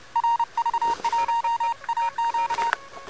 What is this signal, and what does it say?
Morse code, heard as a keyed tone, about 31 wpm, 910 Hz. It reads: RV1AQ
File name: Expedition / 12.01.2013 RK3WWZ UA1CEG p test